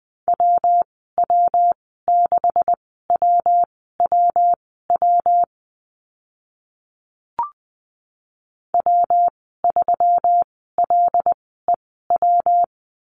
Morse code, heard as a keyed tone, about 20 wpm, spaced as for 14 wpm, 700 Hz. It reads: WW6WWW W3LEW
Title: 140 Lesson 23g - Letter f - Call Signs 20wpm